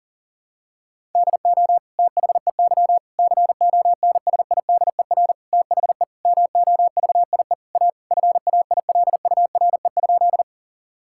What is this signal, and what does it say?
Morse code, heard as a keyed tone, about 37 wpm, 700 Hz. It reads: DO THEY CONSIDER THE MOVIE A FAILURE?
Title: Sentences from Top 1000 Words - Review Part 6 - 40wpm